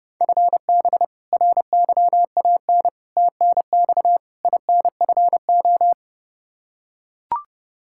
Morse code, heard as a keyed tone, about 29 wpm, 700 Hz. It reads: FB RYAN TNX INFO
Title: K1USN SST QSO Practice 30wpm